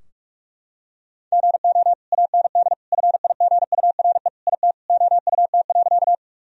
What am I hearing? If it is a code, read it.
Morse code, heard as a keyed tone, about 46 wpm, 700 Hz. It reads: GO AND FIGURE IT OUT.
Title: Sentences from Top 1000 Words - Review Part 2 - 45wpm